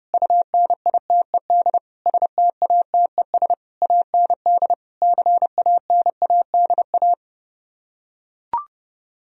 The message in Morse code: UNITED STATES AND CANADA

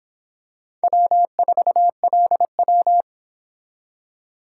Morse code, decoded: W4LW

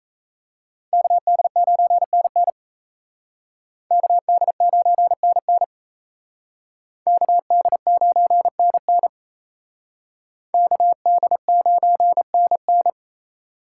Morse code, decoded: KD9NN KD9NN KD9NN KD9NN

